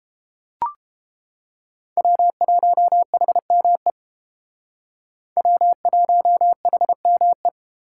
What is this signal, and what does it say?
Morse code, heard as a keyed tone, about 31 wpm, 700 Hz. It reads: W1HME W1HME